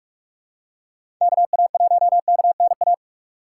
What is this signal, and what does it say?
Morse code, heard as a keyed tone, about 46 wpm, 700 Hz. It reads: KA1KNA
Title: USA Call Signs - 6 Characters - 3x Character Spacing - 45wpm